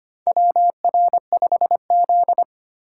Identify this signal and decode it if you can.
Morse code, heard as a keyed tone, about 25 wpm, 700 Hz. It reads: WR5Z
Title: USA Call Signs - 4 Characters - 2x Character Spacing - 25wpm